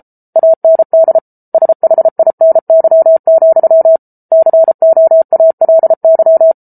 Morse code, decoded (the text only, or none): AND SHINY, COALY